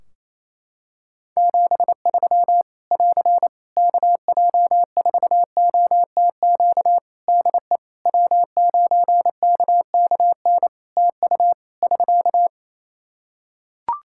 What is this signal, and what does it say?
Morse code, heard as a keyed tone, about 27 wpm, 700 Hz. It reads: 73 <AR> KJ4OTQ DE W9KKN TU <SK>
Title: Practice QSO - Exchange 4 - Ending - 28wpm